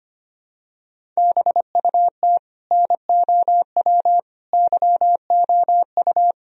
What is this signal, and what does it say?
Morse code, heard as a keyed tone, about 25 wpm, 700 Hz. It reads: BUT NOW YOU
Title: Sentences from Top 1000 Words - Review Part 1 - 25wpm